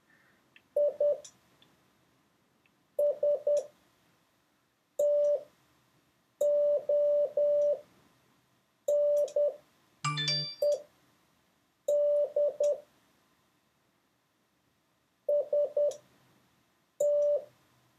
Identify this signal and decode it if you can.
Morse code, heard as a keyed tone, about 10 wpm, spaced as for 5 wpm, 580 Hz. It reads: I STONED ST